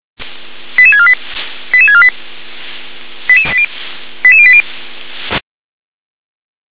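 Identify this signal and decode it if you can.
Five-tone selective call (ZVEI), heard as quick runs of five tones, tones of about 70 ms. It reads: 88648 88648 88080
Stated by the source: ZVEI five-tone selective call received on SDR.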